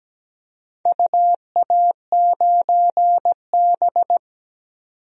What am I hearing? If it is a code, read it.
Morse code, heard as a keyed tone, about 17 wpm, 700 Hz. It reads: UA9B
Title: DXCC-100 Call Signs - 4 Characters - 17wpm